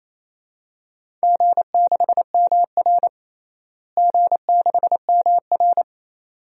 Morse code, heard as a keyed tone, about 27 wpm, 700 Hz. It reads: G6MR G6MR
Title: DXCC-100 Call Signs - 4 Characters - Repeated 2x - 28wpm